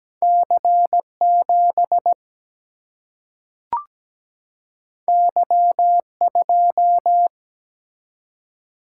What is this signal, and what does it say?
Morse code, heard as a keyed tone, about 17 wpm, 700 Hz. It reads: C7 Y2
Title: Single-Letter Single-Number 17wpm